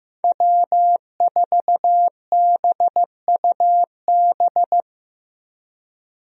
Morse code, heard as a keyed tone, about 15 wpm, 700 Hz. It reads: W4BUB
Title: USA Call Signs - 5 Characters - Repeated 3x - 15wpm